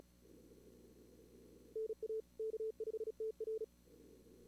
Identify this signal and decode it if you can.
Morse code, heard as a keyed tone, about 36 wpm, 440 Hz. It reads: NA K5TR
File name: CWSprint / Feb2022 / k5tr-feb-na-sprint-cw-k5ot -2300